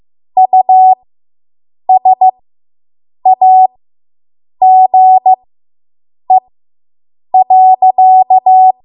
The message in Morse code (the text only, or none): USAGE.